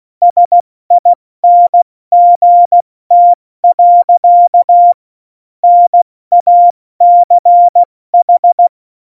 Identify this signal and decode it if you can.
Morse code, heard as a keyed tone, about 16 wpm, 700 Hz. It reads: SINGT. NACH
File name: Morsekurs / Witze / Witz 17. Oktober 17, Tempo 70